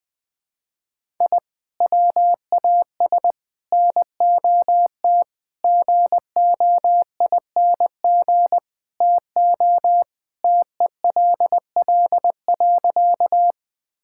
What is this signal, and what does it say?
Morse code, heard as a keyed tone, about 20 wpm, 700 Hz. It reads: I WAS NOT GOING TO TELL.
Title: Sentences with 5-Word Phrases - Spoken Beginning - 20wpm